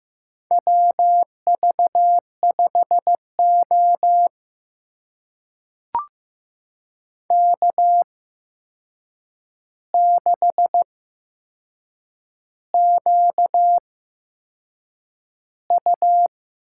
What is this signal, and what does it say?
Morse code, heard as a keyed tone, about 15 wpm, spaced as for 6 wpm, 700 Hz. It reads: WV5O K6QU
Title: USA Call Signs - 4 Characters - 8x Character Spacing - 15wpm